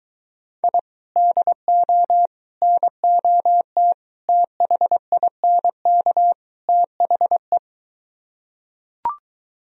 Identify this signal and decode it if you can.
Morse code, heard as a keyed tone, about 23 wpm, 700 Hz. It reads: I DO NOT THINK THE